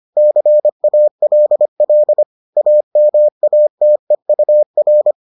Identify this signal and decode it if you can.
Morse code, heard as a keyed tone, about 25 wpm, 600 Hz. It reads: CALL AMATEUR